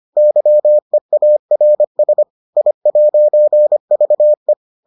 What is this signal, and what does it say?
Morse code, heard as a keyed tone, about 25 wpm, 600 Hz. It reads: YEARS I'VE